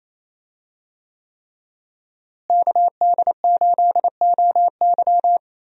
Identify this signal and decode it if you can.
Morse code, heard as a keyed tone, about 28 wpm, 700 Hz. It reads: KD8OY
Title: USA Call Signs - 5 Characters - Repeated 3x - 28wpm